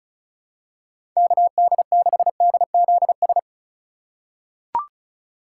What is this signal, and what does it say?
Morse code, heard as a keyed tone, about 36 wpm, 700 Hz. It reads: KD6DZS
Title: USA Call Signs - 6 Characters - 3x Character Spacing - 35wpm